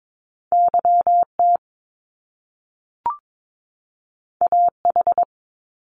Morse code, heard as a keyed tone, about 22 wpm, 700 Hz. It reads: YT AH
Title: Top Two Letter Combinations in Words 22wpm